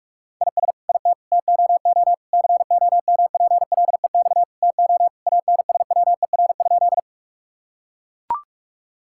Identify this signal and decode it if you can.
Morse code, heard as a keyed tone, about 41 wpm, 700 Hz. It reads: IS IT TOO COMPLEX TO ANSWER?
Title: Sentences from Top 1000 Words - Review Part 6 - 45wpm